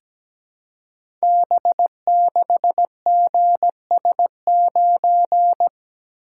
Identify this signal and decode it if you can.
Morse code, heard as a keyed tone, about 17 wpm, 700 Hz. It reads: B6GS9
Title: Code Groups - 5 Letter:Numbers - 17wpm